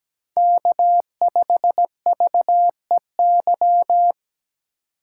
Morse code, heard as a keyed tone, about 17 wpm, 700 Hz. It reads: K5VEY
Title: USA Call Signs - 5 Characters - 6x Character Spacing - 17wpm